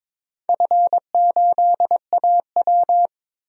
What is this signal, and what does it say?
Morse code, heard as a keyed tone, about 22 wpm, 700 Hz. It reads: F8AW
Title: DXCC-100 Call Signs - 4 Characters - 22wpm